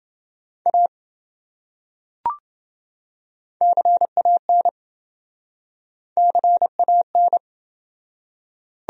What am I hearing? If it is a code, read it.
Morse code, heard as a keyed tone, about 27 wpm, 700 Hz. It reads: A CAN CAN